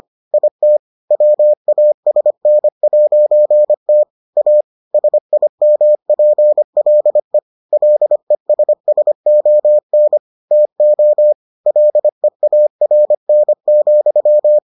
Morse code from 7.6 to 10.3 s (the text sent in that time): LESSON